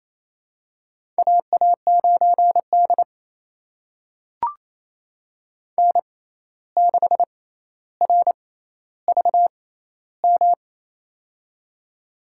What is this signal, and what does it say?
Morse code, heard as a keyed tone, about 28 wpm, spaced as for 13 wpm, 700 Hz. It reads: AA9D N6RVM
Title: USA Call Signs - 6x Character Spacing - 28wpm